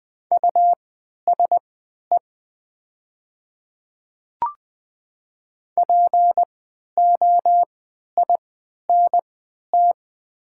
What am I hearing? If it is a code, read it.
Morse code, heard as a keyed tone, about 20 wpm, spaced as for 11 wpm, 700 Hz. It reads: USE POINT
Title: 3x Character Spacing - Top 200 Words - 20wpm